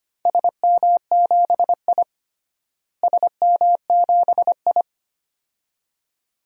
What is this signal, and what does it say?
Morse code, heard as a keyed tone, about 25 wpm, 700 Hz. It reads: SM7I SM7I
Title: DXCC-100 Call Signs - 4 Characters - Repeated 2x - 25wpm